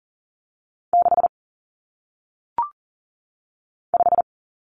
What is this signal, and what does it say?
Morse code, heard as a keyed tone, about 40 wpm, 700 Hz. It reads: 6 5
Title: Common Cut Numbers - 1-Digit 40wpm